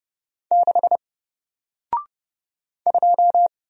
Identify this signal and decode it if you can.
Morse code, heard as a keyed tone, about 30 wpm, 700 Hz. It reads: 6 2
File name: Single-Letter-Number-30wpm